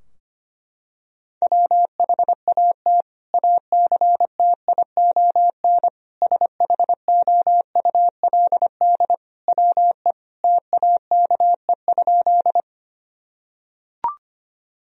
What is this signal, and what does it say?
Morse code, heard as a keyed tone, about 25 wpm, 700 Hz. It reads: WHAT ACTION SHOULD WE TAKE?
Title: Sentences from Top 1000 Words - Review Part 4 - 25wpm